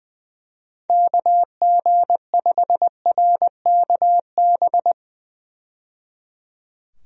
Morse code, heard as a keyed tone, about 21 wpm, 700 Hz. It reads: KG5RKB
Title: USA Call Signs - 6 Characters - 20wpm